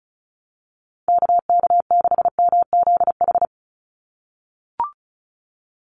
Morse code, heard as a keyed tone, about 34 wpm, 700 Hz. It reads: KK6MZH
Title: USA Call Signs 35wpm